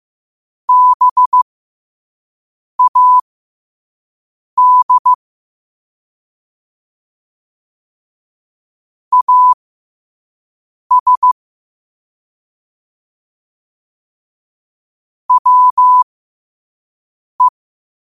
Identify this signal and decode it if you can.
Morse code, heard as a keyed tone, about 15 wpm, spaced as for 5 wpm, 1000 Hz. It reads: BAD AS WE
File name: qotdfiles / 5wpm